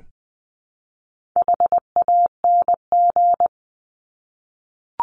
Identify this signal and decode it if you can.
Morse code, heard as a keyed tone, about 20 wpm, 700 Hz. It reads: HANG
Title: Top 1501-1750 Words 20wpm